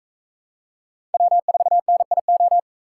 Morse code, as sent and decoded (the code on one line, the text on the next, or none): .-- ....- -. .. ---
W4NIO